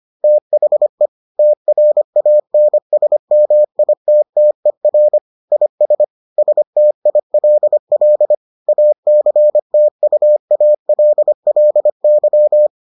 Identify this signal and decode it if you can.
Morse code, heard as a keyed tone, about 25 wpm, 600 Hz. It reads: THE TRANSMITTER IS STILL ACTUALLY